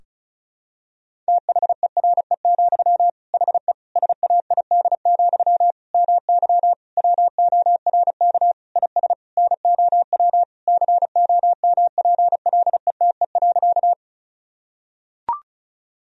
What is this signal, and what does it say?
Morse code, heard as a keyed tone, about 34 wpm, 700 Hz. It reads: THERE, HE SAID, MY WORK IS NOW COMPLETE.
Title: Sentences from Top 1000 Words - Review Part 2 - 35wpm